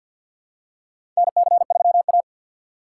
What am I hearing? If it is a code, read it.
Morse code, heard as a keyed tone, about 53 wpm, 700 Hz. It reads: NC3A
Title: USA Call Signs - 6x Character Spacing - 50wpm